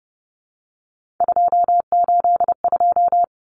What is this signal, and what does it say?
Morse code, heard as a keyed tone, about 30 wpm, 700 Hz. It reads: 282
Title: Numbers - 3-Digit 30wpm